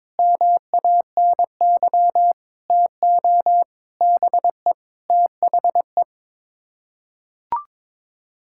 Morse code, heard as a keyed tone, about 22 wpm, 700 Hz. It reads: MANY TO BE THE
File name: practice / 22wpm / Sets of 7 Words - 22wpm